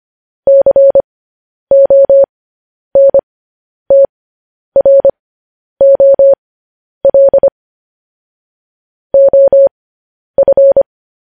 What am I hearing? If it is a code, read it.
Morse code, heard as a keyed tone, about 25 wpm, spaced as for 10 wpm, 550 Hz. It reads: CONTROL OF
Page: news headlines in morse code at 10 wpm